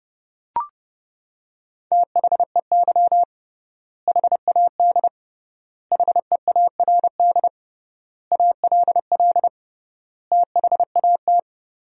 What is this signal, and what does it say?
Morse code, heard as a keyed tone, about 30 wpm, 700 Hz. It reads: THEY HAD HEARD ALL THAT
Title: 3x Word-Spacing - Sentences from Top 400 Words - 30wpm